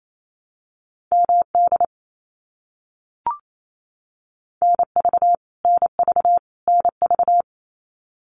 Morse code, heard as a keyed tone, about 28 wpm, 700 Hz. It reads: MD NV NV NV
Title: US States Repeated 3x 28wpm